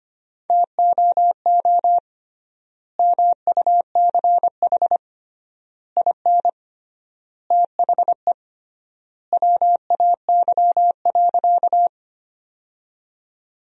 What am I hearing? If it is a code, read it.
Morse code, heard as a keyed tone, about 25 wpm, 700 Hz. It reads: TOO MUCH IN THE WAY.
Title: 3x Word-Spacing - Sentences from Top 200 Words - 25wpm